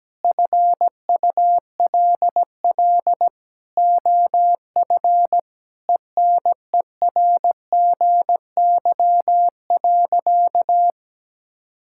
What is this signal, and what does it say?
Morse code, heard as a keyed tone, about 17 wpm, 700 Hz. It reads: FULL OF ENERGY.
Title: Sentences from Top 1000 Words - Review Part 6 - 17wpm